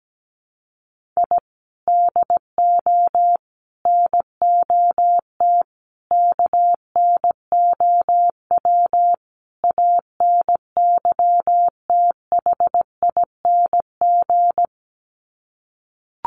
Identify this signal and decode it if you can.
Morse code, heard as a keyed tone, about 17 wpm, 700 Hz. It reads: I DO NOT KNOW ANYTHING
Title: Sets of 5 Words Encore 17wpm